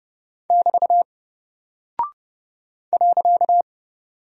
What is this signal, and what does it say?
Morse code, heard as a keyed tone, about 30 wpm, 700 Hz. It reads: <BT> .
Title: Single Character (Letter:Number:Punctuation:Prosign) Alt 30wpm